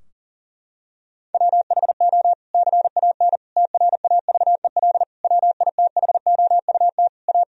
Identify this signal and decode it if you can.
Morse code, heard as a keyed tone, about 40 wpm, 700 Hz. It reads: WHO CAN TRAVEL WITHOUT A
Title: Sentences from Top 1000 Words - Review Part 3 - 40wpm